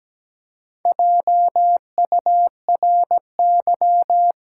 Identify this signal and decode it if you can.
Morse code, heard as a keyed tone, about 17 wpm, 700 Hz. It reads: JURY
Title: Four Letter Words - 17wpm